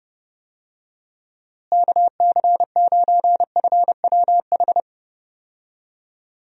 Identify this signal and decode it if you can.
Morse code, heard as a keyed tone, about 30 wpm, 700 Hz. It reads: KC9FWH
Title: USA Call Signs - 6 Characters - Repeated 2x - 30wpm